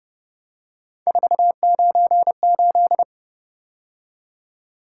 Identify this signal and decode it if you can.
Morse code, heard as a keyed tone, about 29 wpm, 700 Hz. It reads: 498